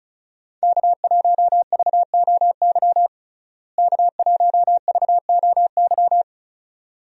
Morse code, heard as a keyed tone, about 36 wpm, 700 Hz. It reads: K1VOY K1VOY
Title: USA Call Signs - 5 Characters - Repeated 2x - 35wpm